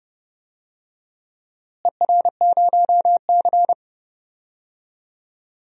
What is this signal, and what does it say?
Morse code, heard as a keyed tone, about 28 wpm, 700 Hz. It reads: ER0C
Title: DXCC-100 Call Signs - 4 Characters - Repeated 3x - 30wpm